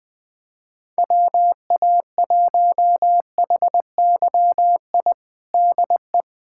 Morse code, heard as a keyed tone, about 20 wpm, 700 Hz. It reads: WA1HYI DE
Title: USA Callsign DE Callsign - 6 Characters - 20wpm